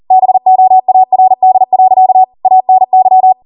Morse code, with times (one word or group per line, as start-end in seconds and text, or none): 0.1-2.2 BOARD.
2.5-3.3 ANY